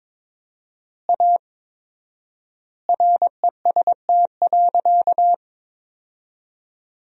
Morse code, heard as a keyed tone, about 22 wpm, 700 Hz. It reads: A REST.
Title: 4x Word-Spacing - Sentences from Top 500 Words - 22wpm